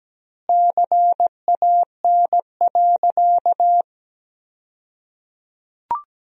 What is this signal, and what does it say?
Morse code, heard as a keyed tone, about 17 wpm, 700 Hz. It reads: CAN.
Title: 2x Word-Spacing - Sentences from Top 500 Words - 17wpm